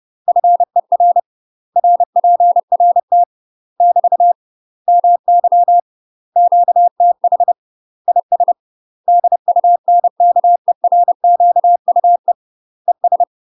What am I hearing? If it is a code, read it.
Morse code, heard as a keyed tone, about 30 wpm, 700 Hz. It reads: FER RPRT <BT> MY QTH IS DUNKERQUE ES
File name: UK MASTER QSOs TRAINING / 30WPM / UK MASTER QSOs 0029